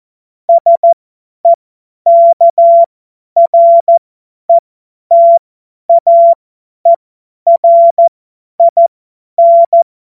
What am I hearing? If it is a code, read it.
Morse code, heard as a keyed tone, about 14 wpm, spaced as for 10 wpm, 680 Hz. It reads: SEKRETAERIN
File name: Morsekurs / Witze / Witz 06. Dezember 2022 50 F70